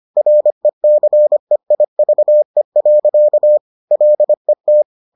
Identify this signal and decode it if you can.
Morse code, heard as a keyed tone, about 25 wpm, 600 Hz. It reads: RECEIVE. LET